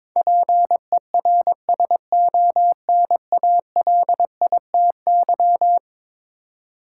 Morse code, heard as a keyed tone, about 22 wpm, 700 Hz. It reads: PERSONALITY
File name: practice / 22wpm / Eleven Letter Words - 22wpm